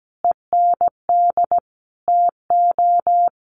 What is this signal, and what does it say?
Morse code, heard as a keyed tone, about 17 wpm, 700 Hz. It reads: END TO
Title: Sentences from Top 1100 Words 17wpm